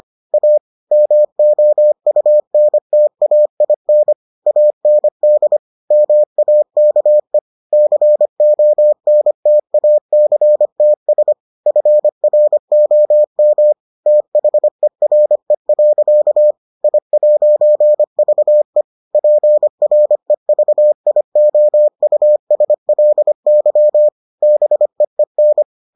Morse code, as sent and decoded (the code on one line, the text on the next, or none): .- / -- --- ..- -. - .- .. -. / .- -. -.. / -- .- -.- . / -.-. --- -. - .- -.-. - ... / ..-. .-. --- -- / - .... .-. .-.-.- / .. .----. ...- . / .--. .-. ...- .. --- ..- ... .-.. -.-- / -... . -.
A MOUNTAIN AND MAKE CONTACTS FROM THERE. I'VE PREVIOUSLY BEEN